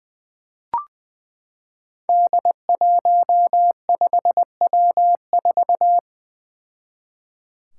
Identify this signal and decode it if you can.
Morse code, heard as a keyed tone, about 20 wpm, 700 Hz. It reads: D15W4
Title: Five Challenging Letter-Number Combinations 20wpm